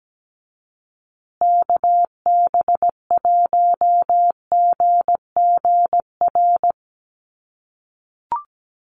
Morse code, heard as a keyed tone, about 17 wpm, 700 Hz. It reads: KB1GGR
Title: USA Call Signs Repeated 3x 17wpm